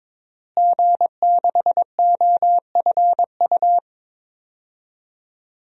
Morse code, heard as a keyed tone, about 22 wpm, 700 Hz. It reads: G6OFU
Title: DXCC-100 Call Signs - 5 Characters - Repeated 2x - 22wpm